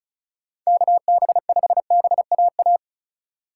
Morse code, heard as a keyed tone, about 44 wpm, 700 Hz. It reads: KB5BAA